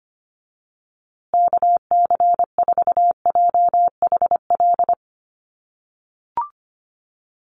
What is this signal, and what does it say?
Morse code, heard as a keyed tone, about 25 wpm, 700 Hz. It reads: KC4JHL
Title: USA Call Signs Repeated 2x 25wpm